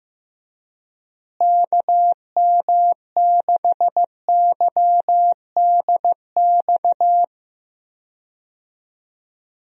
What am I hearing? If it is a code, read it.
Morse code, heard as a keyed tone, about 15 wpm, 700 Hz. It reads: KM6YDX
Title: USA Call Signs - 6 Characters - Repeated 3x - 15wpm